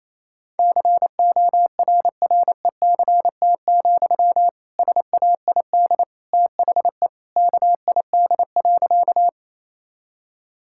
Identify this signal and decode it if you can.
Morse code, heard as a keyed tone, about 28 wpm, 700 Hz. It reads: CORRECT, SAID THE KID.
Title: Sentences from Top 1600 Words 28wpm